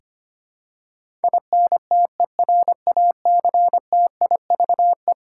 Morse code, heard as a keyed tone, about 25 wpm, 700 Hz. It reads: INTERACTIVE
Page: morse-code-ninja-practice-25wpm